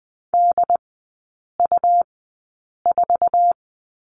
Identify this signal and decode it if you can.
Morse code, heard as a keyed tone, about 20 wpm, spaced as for 8 wpm, 700 Hz. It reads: DU4
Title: 100 Lesson 17e - Letter u - Three Characters 20wpm